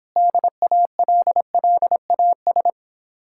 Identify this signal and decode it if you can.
Morse code, heard as a keyed tone, about 26 wpm, 700 Hz. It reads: DALLAS